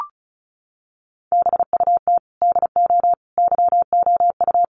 Morse code, heard as a keyed tone, about 34 wpm, 700 Hz. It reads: BUT DO YOU
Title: Sentences from Top 600 Words 35wpm